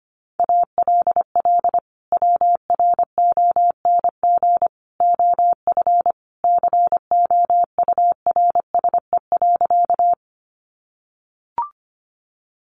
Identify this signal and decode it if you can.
Morse code, heard as a keyed tone, about 25 wpm, 700 Hz. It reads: ALL WRONG OF COURSE.
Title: Sentences from Top 900 Words 25wpm